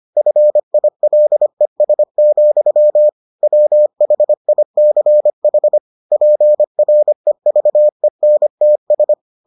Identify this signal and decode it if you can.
Morse code, heard as a keyed tone, about 25 wpm, 600 Hz. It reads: FILES, WHICH PREVENTS